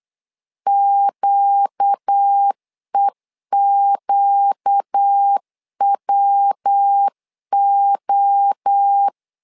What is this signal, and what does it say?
Morse code, heard as a keyed tone, about 8 wpm, 800 Hz. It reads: QEQWO
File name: 4 Q 8wpm